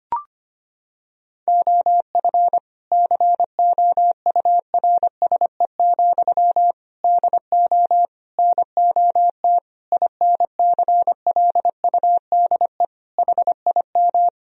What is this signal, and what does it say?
Morse code, heard as a keyed tone, about 25 wpm, 700 Hz. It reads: OF COURSE, DO NOT INCLUDE HIM
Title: Sentences from Top 1000 Words - Review Part 3 - 25wpm